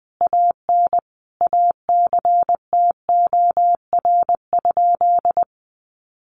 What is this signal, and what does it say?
Morse code, heard as a keyed tone, about 20 wpm, 700 Hz. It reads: AN ACTOR?
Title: Sentences from Top 1500 Words 20wpm